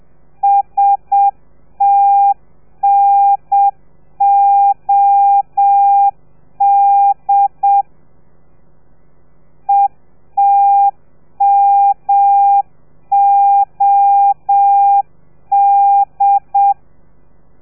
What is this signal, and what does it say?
Morse code, heard as a keyed tone, about 7 wpm, 800 Hz. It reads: STNOD ETMOD